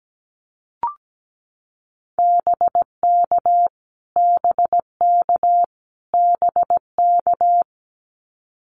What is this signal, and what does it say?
Morse code, heard as a keyed tone, about 17 wpm, 700 Hz. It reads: BK BK BK
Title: Abbreviations Repeated 3x - 17wpm